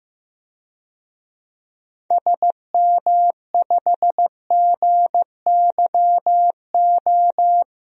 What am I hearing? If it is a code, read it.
Morse code, heard as a keyed tone, about 15 wpm, 700 Hz. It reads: SM5GYO
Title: DXCC-100 Call Signs - 6 Characters - 15wpm